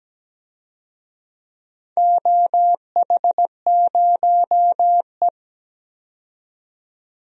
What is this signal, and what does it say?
Morse code, heard as a keyed tone, about 17 wpm, 700 Hz. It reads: OH0E
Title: DXCC-100 Call Signs - 4 Characters - 17wpm